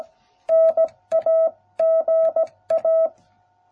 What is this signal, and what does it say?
Morse code, heard as a keyed tone, about 17 wpm, 660 Hz. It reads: NAGA